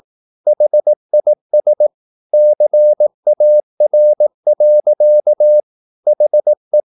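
Morse code, heard as a keyed tone, about 18 wpm, 600 Hz. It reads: HIS CAR. HE
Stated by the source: Morse practice at 18 wpm, English text, from CW med Gnister.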